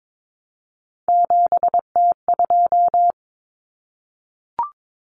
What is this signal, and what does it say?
Morse code, heard as a keyed tone, about 22 wpm, 700 Hz. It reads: 7T2
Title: Common Cut Numbers - 3-Digit 22wpm